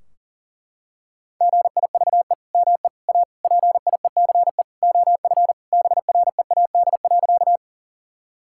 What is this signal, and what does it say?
Morse code, heard as a keyed tone, about 40 wpm, 700 Hz. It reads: GIVE ME A PIECE OF BREAD.
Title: Sentences from Top 1600 Words 40wpm